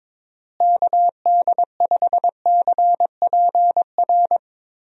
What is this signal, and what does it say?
Morse code, heard as a keyed tone, about 22 wpm, 700 Hz. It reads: KD5CPR
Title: USA Call Signs - 6 Characters - Repeated 2x - 22wpm